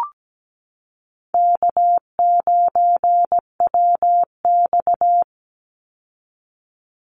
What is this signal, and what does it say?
Morse code, heard as a keyed tone, about 17 wpm, 700 Hz. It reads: K9WX
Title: Easy Call Signs Repeated 3x - 17wpm